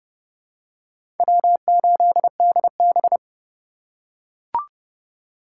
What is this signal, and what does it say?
Morse code, heard as a keyed tone, about 30 wpm, 700 Hz. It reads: W8DB
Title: USA Call Signs - 4 Characters - 4x Character Spacing - 30wpm